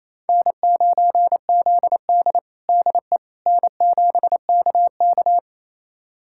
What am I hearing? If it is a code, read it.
Morse code, heard as a keyed tone, about 28 wpm, 700 Hz. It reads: N9ZD DE N7KK
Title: USA Callsign DE Callsign - 4 Characters - 28wpm